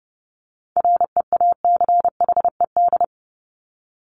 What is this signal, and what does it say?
Morse code, heard as a keyed tone, about 30 wpm, 700 Hz. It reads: REACHED
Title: ED Suffix - Top 100 Words with Base Words 30wpm